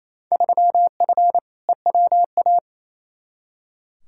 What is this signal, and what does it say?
Morse code, heard as a keyed tone, about 28 wpm, 700 Hz. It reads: 3F EWA
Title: Field Day Exchanges Encore - 28wpm